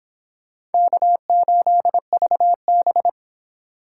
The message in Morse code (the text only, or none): K8VB